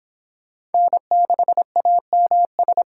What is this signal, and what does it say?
Morse code, heard as a keyed tone, about 29 wpm, 700 Hz. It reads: N6AMS